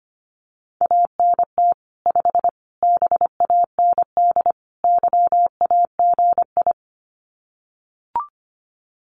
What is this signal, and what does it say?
Morse code, heard as a keyed tone, about 25 wpm, 700 Hz. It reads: ANT 5 BAND YAGI
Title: Common QSO Phrases 25wpm